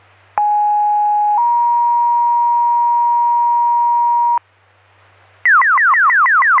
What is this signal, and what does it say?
Two-tone page, heard as a one-second tone, then a three-second tone, about 830 then 950 Hz. It is heard from 0.4 to 4.4 s.